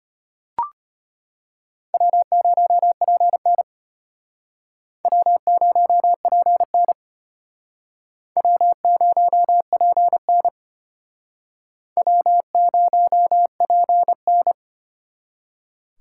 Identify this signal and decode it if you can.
Morse code, heard as a keyed tone, about 31 wpm, 700 Hz. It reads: W0PN W0PN W0PN W0PN